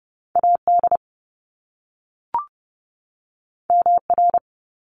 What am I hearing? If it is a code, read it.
Morse code, heard as a keyed tone, about 30 wpm, 700 Hz. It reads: AD MR
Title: Top Two Letter Combinations in Words 30wpm